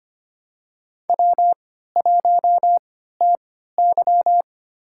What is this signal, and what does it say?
Morse code, heard as a keyed tone, about 25 wpm, spaced as for 14 wpm, 700 Hz. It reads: W1TY
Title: USA Call Signs - 4 Characters - 3x Character Spacing - 25wpm